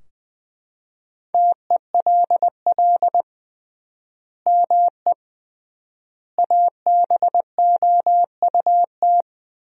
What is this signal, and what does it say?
Morse code, heard as a keyed tone, about 20 wpm, 700 Hz. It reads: TELL ME ABOUT
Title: 3x Word-Spacing - Sentences from Top 200 Words - 20wpm